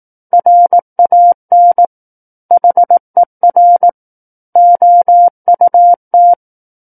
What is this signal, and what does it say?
Morse code, heard as a keyed tone, about 18 wpm, 700 Hz. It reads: RAN HER OUT
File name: 18 WPM